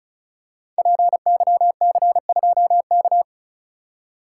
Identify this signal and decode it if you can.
Morse code, heard as a keyed tone, about 36 wpm, 700 Hz. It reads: PYC2K
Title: Code Groups - 5 Letter:Numbers - 35wpm